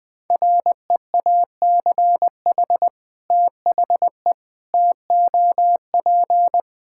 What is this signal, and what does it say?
Morse code, heard as a keyed tone, about 20 wpm, 700 Hz. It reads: REACH THE TOP
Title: Sentences with 2-Word Phrases - 20wpm